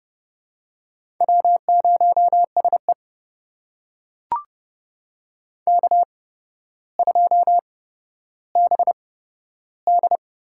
Morse code, heard as a keyed tone, about 30 wpm, spaced as for 13 wpm, 700 Hz. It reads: W0SE K2BD
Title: USA Call Signs - 4 Characters - 8x Character Spacing - 30wpm